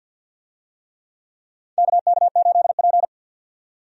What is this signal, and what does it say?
Morse code, heard as a keyed tone, about 52 wpm, 700 Hz. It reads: KK8P
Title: USA Call Signs - 4 Characters - 50wpm